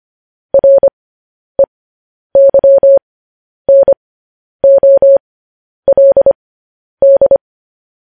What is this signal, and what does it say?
Morse code, heard as a keyed tone, about 25 wpm, spaced as for 10 wpm, 550 Hz. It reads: REYNOLD
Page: News Headlines in Morse Code at 10 WPM